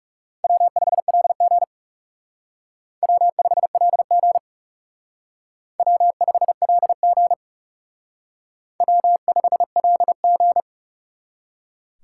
Morse code, heard as a keyed tone, about 37 wpm, 700 Hz. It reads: W5LG W5LG W5LG W5LG
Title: Speed Racing - USA Call Signs 30wpm